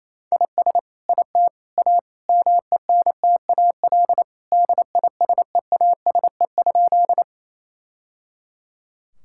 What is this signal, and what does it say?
Morse code, heard as a keyed tone, about 28 wpm, 700 Hz. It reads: IS IT A MENTAL DISEASE?
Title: Sentences from Top 1000 Words - Review Part 6 - 28wpm